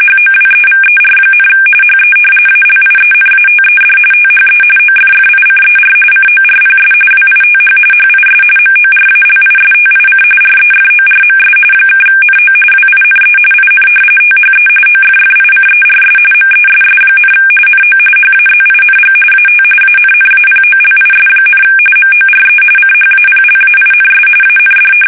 RTTY, about 45.45 baud. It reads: HOWEVER, HE ALIENATED THE OFFICER CORPS WITH HIS MORALISTIC REFORMS (NO WINE IN THE OFFICERS MESS, NO HAZING AT ANNAPOLIS, MORE CHAPLAINS AND
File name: fsk850